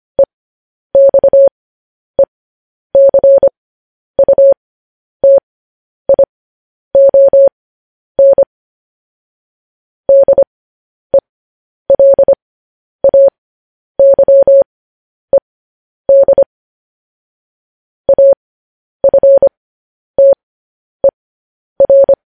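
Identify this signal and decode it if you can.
Morse code, heard as a keyed tone, about 25 wpm, spaced as for 10 wpm, 550 Hz. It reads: EXECUTION DELAYED AFTER
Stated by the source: Morse code news headlines sent as a reading transcription, preceded by vvv.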